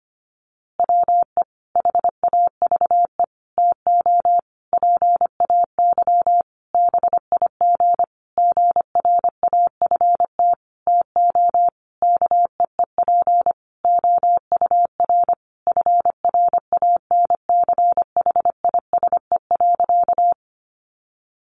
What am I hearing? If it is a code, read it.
Morse code, heard as a keyed tone, about 25 wpm, 700 Hz. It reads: WE HAVE TO PAY BIG GRAFT TO KEEP OUR FRANCHISE.
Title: A Sound Of Thunder 25wpm